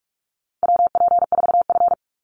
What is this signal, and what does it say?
Morse code, heard as a keyed tone, about 44 wpm, 700 Hz. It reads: WP4F
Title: Call Signs - 45wpm